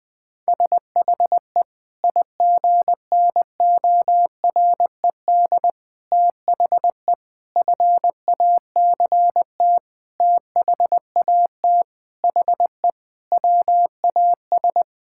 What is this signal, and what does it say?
Morse code, heard as a keyed tone, about 20 wpm, 700 Hz. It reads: SHE IGNORED THE FACT THAT HE WAS
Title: Sentences with 5-Word Phrases - Spoken Beginning - 20wpm